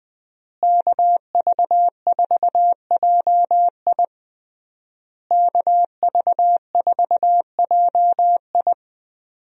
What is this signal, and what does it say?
Morse code, heard as a keyed tone, about 21 wpm, 700 Hz. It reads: KV4JI KV4JI
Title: USA Call Signs - 5 Characters - Repeated 3x - 20wpm